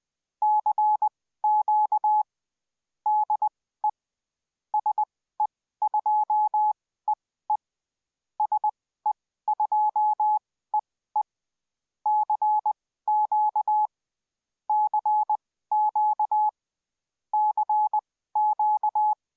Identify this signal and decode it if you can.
Morse code, heard as a keyed tone, about 20 wpm, spaced as for 14 wpm, 850 Hz. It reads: CQ DE SE2EE SE2EE CQ CQ CQ